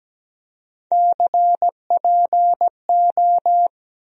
Morse code, heard as a keyed tone, about 17 wpm, 700 Hz. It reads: CPO
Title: USA Call Sign Postfixes - 3 Letters 17wpm